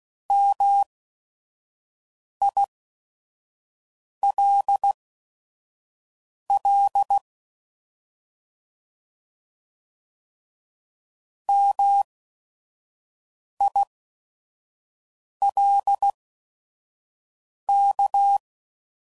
Morse code, heard as a keyed tone, about 16 wpm, spaced as for 5 wpm, 800 Hz. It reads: MILL MILK